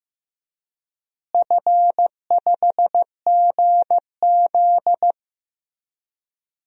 Morse code, heard as a keyed tone, about 15 wpm, 700 Hz. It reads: F5GZ
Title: DXCC-100 Call Signs - 4 Characters - Repeated 3x - 15wpm